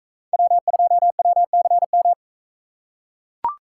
Morse code, heard as a keyed tone, about 42 wpm, 700 Hz. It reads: W2WCM